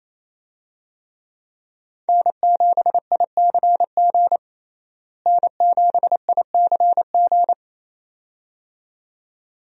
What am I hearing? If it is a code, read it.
Morse code, heard as a keyed tone, about 28 wpm, 700 Hz. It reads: N7ICG N7ICG
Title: USA Call Signs - 5 Characters - Repeated 2x - 28wpm